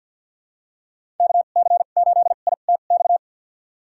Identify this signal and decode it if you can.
Morse code, heard as a keyed tone, about 53 wpm, spaced as for 37 wpm, 690 Hz. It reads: KC8ITX
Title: USA Call Signs - 2x Character Spacing - 50wpm